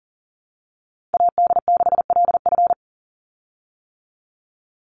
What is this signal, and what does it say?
Morse code, heard as a keyed tone, about 40 wpm, 700 Hz. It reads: AD6LF
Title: USA Call Signs 40wpm